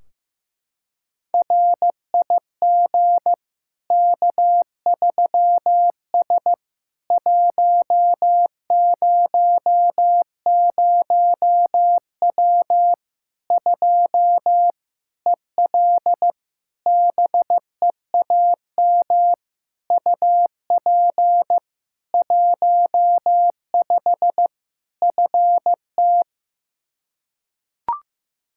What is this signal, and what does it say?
Morse code, heard as a keyed tone, about 15 wpm, 700 Hz. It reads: RIG K3S 100W 2 EL BEAM UP 15 FT
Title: Practice QSO - Exchange 2 - Info - 15wpm